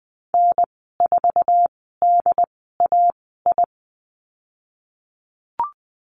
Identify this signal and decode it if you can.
Morse code, heard as a keyed tone, about 20 wpm, spaced as for 15 wpm, 700 Hz. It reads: N4DAI
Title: 076 Lesson 13g - Letter L - Call Signs 20wpm